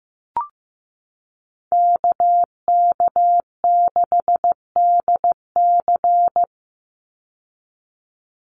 Morse code, heard as a keyed tone, about 15 wpm, 700 Hz. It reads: KK6DC
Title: USA Call Signs 15wpm